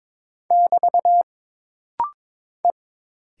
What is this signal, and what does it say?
Morse code, heard as a keyed tone, about 22 wpm, 700 Hz. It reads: <BT> E
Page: This rapid-fire timing is geared to push your ICR (Instant Character Recognition) skills.